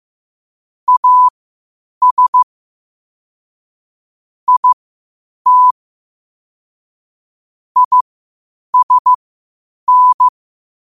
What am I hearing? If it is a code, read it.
Morse code, heard as a keyed tone, about 15 wpm, spaced as for 8 wpm, 1000 Hz. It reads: AS IT ISN
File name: qotdfiles / 8wpm